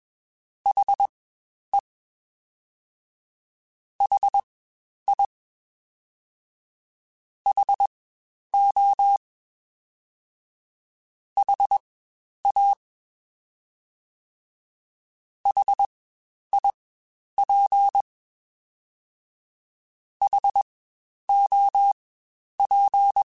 Morse code, heard as a keyed tone, about 21 wpm, spaced as for 10 wpm, 800 Hz. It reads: HE HI HO HA HIP HOP
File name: Spaced Alphas / Spaced Alphas mp3s / Hs sp @20